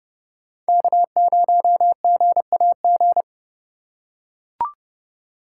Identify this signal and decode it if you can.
Morse code, heard as a keyed tone, about 30 wpm, 700 Hz. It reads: K0GAG